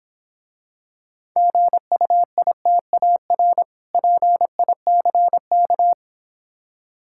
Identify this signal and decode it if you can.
Morse code, heard as a keyed tone, about 26 wpm, 700 Hz. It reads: GUITAR PICK